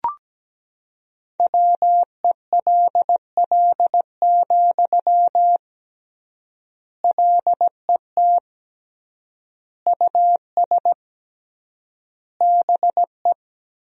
Morse code, heard as a keyed tone, about 17 wpm, 700 Hz. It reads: WELL, LET US BE
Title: 3x Word-Spacing - Sentences from Top 300 Words - 17wpm